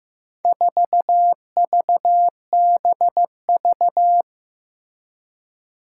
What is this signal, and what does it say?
Morse code, heard as a keyed tone, about 15 wpm, 700 Hz. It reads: 4VBV
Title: Four Challenging Letter-Number Combinations 15wpm